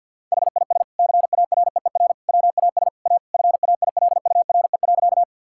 Morse code, heard as a keyed tone, about 50 wpm, 700 Hz. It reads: HIS CAREER WAS A FAILURE.
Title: Sentences from Top 1000 Words - Review Part 6 - 50wpm